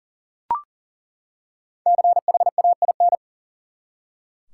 Morse code, heard as a keyed tone, about 37 wpm, 700 Hz. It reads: CHAIN
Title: Five Letter Words - 40wpm